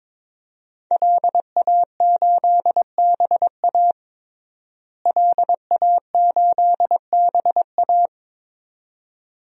Morse code, heard as a keyed tone, about 21 wpm, 700 Hz. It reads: LA8BA LA8BA
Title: DXCC-100 Call Signs - 5 Characters - Repeated 2x - 22wpm